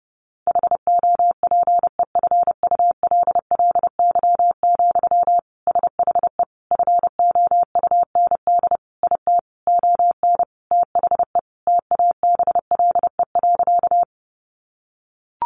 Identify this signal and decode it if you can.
Morse code, heard as a keyed tone, about 28 wpm, 700 Hz. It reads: HOPEFULLY, SHE FOUND IT ON THE TABLE.
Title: Sentences from Top 1500 Words 30wpm